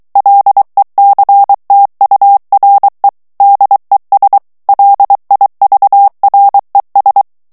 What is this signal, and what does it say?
Morse code, heard as a keyed tone, about 23 wpm, 800 Hz. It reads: LECTURE DES LIVRES
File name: morse mp3 / FRANCAIS TEXTE CLAIR 2 VITESSE 22 MOTS MINUTES